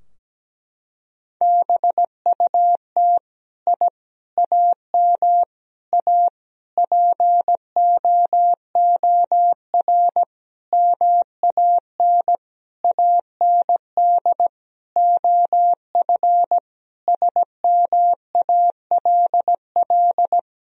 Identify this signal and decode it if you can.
Morse code, heard as a keyed tone, about 17 wpm, 700 Hz. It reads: BUT I AM A POOR MAN AND OF SMALL
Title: Sentences from Top 1000 Words - Review Part 3 - 17wpm